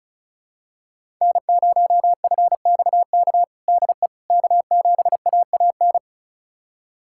Morse code, heard as a keyed tone, about 36 wpm, 700 Hz. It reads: N0FXK DE K7AAN
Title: USA Callsign DE Callsign - 5 Characters - 35wpm